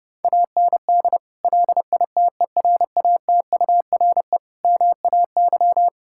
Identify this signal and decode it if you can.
Morse code, heard as a keyed tone, about 30 wpm, 700 Hz. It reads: AND LITERATURE MAY
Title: Sentences from Top 1600 Words 30wpm